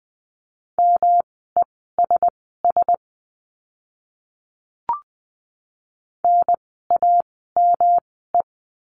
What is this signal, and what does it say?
Morse code, heard as a keyed tone, about 20 wpm, spaced as for 15 wpm, 700 Hz. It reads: MESS NAME
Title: 108 Lesson 18f - Letter m - Words 20wpm